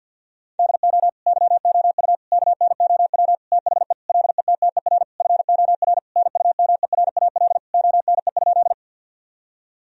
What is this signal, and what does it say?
Morse code, heard as a keyed tone, about 50 wpm, 700 Hz. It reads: DO YOU KNOW THE LETTER FOR NUMERAL ONE?
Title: Sentences from Top 1000 Words - Review Part 2 - 50wpm